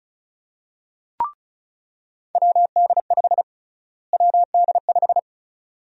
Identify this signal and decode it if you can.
Morse code, heard as a keyed tone, about 34 wpm, 700 Hz. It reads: WD5 WD5
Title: USA Call Sign Prefixes - Repeated 2x - 35wpm